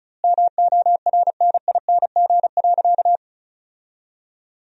Morse code, heard as a keyed tone, about 36 wpm, 700 Hz. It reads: MORNING.
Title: 2x Word-Spacing - Sentences from Top 400 Words - 35wpm